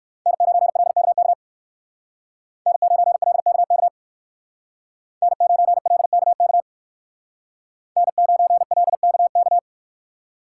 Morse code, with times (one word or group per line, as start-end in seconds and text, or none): 0.3-1.3 N9LKK
2.7-3.9 N9LKK
5.2-6.6 N9LKK
8.0-9.6 N9LKK